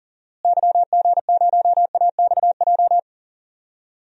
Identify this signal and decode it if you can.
Morse code, heard as a keyed tone, about 39 wpm, 700 Hz. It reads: YG0AXJ
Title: DXCC-100 Call Signs - 6 Characters - Repeated 3x - 40wpm